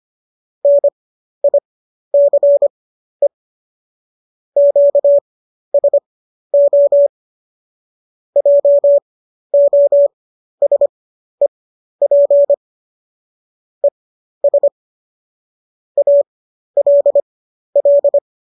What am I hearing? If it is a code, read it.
Morse code, heard as a keyed tone, about 25 wpm, spaced as for 12 wpm, 570 Hz. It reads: NICE QSO JOSEP ES ALL
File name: qso009 12